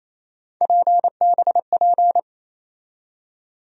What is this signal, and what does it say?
Morse code, heard as a keyed tone, about 28 wpm, 700 Hz. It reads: PBP
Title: USA Call Sign Postfixes - 3 Letters 28wpm